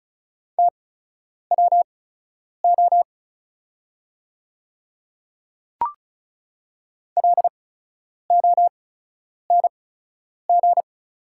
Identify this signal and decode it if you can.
Morse code, heard as a keyed tone, about 35 wpm, spaced as for 10 wpm, 690 Hz. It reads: TWO LONG
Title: 8x Character Spacing - Top 100 Words - 35wpm